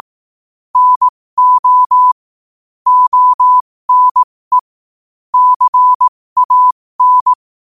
Morse code, heard as a keyed tone, about 18 wpm, spaced as for 15 wpm, 1000 Hz. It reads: NO ONE CAN
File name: qotdfiles / 15wpm